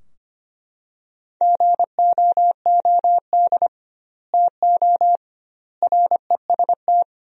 Morse code, heard as a keyed tone, about 25 wpm, 700 Hz. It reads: GOOD TO REST
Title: 2x Word-Spacing - Sentences from Top 500 Words - 25wpm